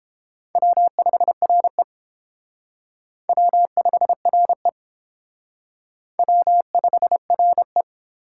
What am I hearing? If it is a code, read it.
Morse code, heard as a keyed tone, about 31 wpm, 700 Hz. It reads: W5RE W5RE W5RE